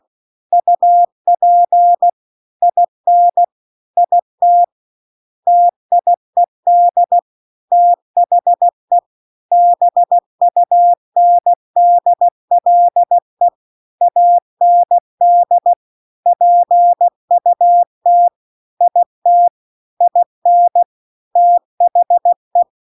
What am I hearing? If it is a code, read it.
Morse code, heard as a keyed tone, about 16 wpm, 700 Hz. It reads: UP IN IT TIED THE BUNDLE AND PUT IT IN THE